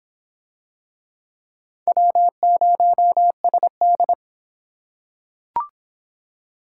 Morse code, heard as a keyed tone, about 26 wpm, 700 Hz. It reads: W0SD